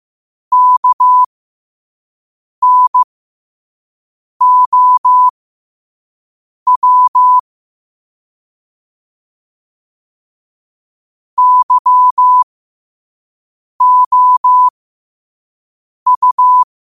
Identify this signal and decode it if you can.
Morse code, heard as a keyed tone, about 15 wpm, spaced as for 5 wpm, 1000 Hz. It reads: KNOW YOU
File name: qotdfiles / 5wpm